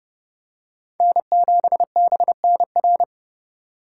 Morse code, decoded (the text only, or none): N7BNR